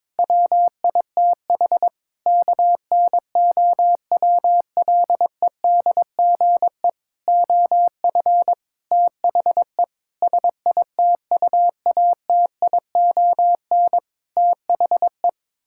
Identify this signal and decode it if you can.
Morse code, heard as a keyed tone, about 22 wpm, 700 Hz. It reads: WITH KNOWLEDGE OF THE SITUATION THE
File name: Sentences with 9-Word Phrases - 22wpm